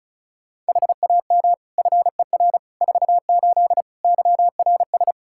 Morse code, heard as a keyed tone, about 36 wpm, 700 Hz. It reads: HAM FER 48 YRS
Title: Practice QSO - Exchange 3 - More Info - 35wpm